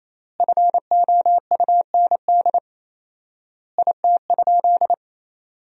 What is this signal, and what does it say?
Morse code, heard as a keyed tone, about 28 wpm, 700 Hz. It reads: FOUND IT?
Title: 4x Word-Spacing - Sentences from Top 300 Words - 28wpm